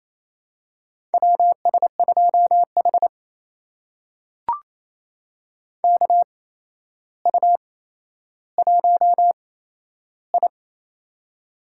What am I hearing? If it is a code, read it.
Morse code, heard as a keyed tone, about 28 wpm, spaced as for 12 wpm, 700 Hz. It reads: WS2H KU1I
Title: USA Call Signs - 4 Characters - 8x Character Spacing - 28wpm